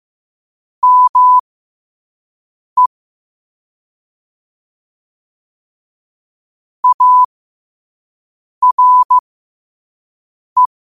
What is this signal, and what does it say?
Morse code, heard as a keyed tone, about 15 wpm, spaced as for 5 wpm, 1000 Hz. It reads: ME ARE
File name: qotdfiles / 5wpm